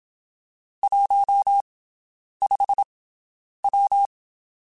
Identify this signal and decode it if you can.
Morse code, heard as a keyed tone, about 27 wpm, spaced as for 9 wpm, 800 Hz. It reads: 15W
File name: (spaced) / QSOs Spaced-mp3s / WD0BGP 1 @25 sp